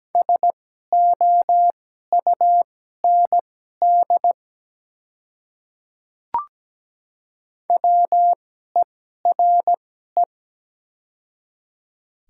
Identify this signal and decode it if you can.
Morse code, heard as a keyed tone, about 17 wpm, spaced as for 12 wpm, 700 Hz. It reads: SOUND WERE
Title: 2x Character Spacing - Top 100 Words - 17wpm